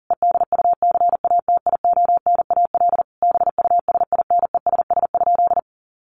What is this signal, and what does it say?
Morse code, heard as a keyed tone, about 40 wpm, 700 Hz. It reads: EDUCATIONAL BUSINESS?
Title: Sentences from Top 1200 Words 40wpm